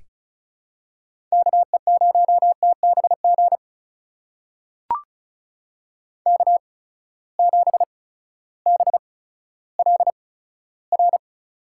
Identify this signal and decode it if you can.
Morse code, heard as a keyed tone, about 36 wpm, spaced as for 16 wpm, 700 Hz. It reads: KE0TBG K7BLR
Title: USA Call Signs - 8x Character Spacing - 35wpm